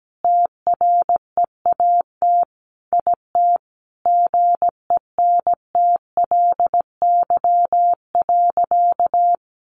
Morse code, heard as a keyed tone, about 17 wpm, 700 Hz. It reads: TREAT IT GENTLY.
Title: Sentences from Top 1500 Words 17wpm